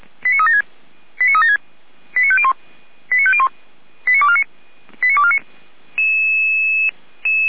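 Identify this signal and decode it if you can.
Five-tone selective call (ZVEI), heard as quick runs of five tones, tones of about 70 ms. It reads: 89376 89376 89501 89501 89249 89249
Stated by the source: ZVEI five-tone selective call, code 89276, received by SDR; fire brigade Feuerwehr Orlamünde.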